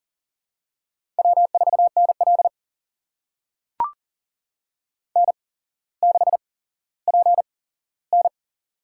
Morse code, heard as a keyed tone, about 38 wpm, spaced as for 17 wpm, 700 Hz. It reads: W4NL N6PN
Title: USA Call Signs - 4 Characters - 8x Character Spacing - 40wpm